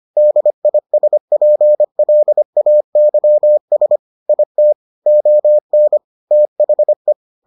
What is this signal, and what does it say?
Morse code, heard as a keyed tone, about 25 wpm, 600 Hz. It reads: DISPLAYS IT ON THE